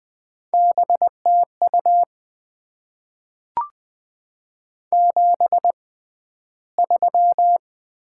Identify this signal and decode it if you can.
Morse code, heard as a keyed tone, about 20 wpm, 700 Hz. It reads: B T U 7 3
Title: Top 20 QSO Elements - 6x Character Spacing - 20wpm